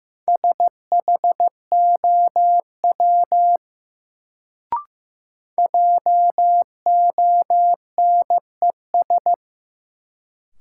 Morse code, heard as a keyed tone, about 15 wpm, 700 Hz. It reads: SHOW JONES
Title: Sets of 5 Words - New Word Drill - 15wpm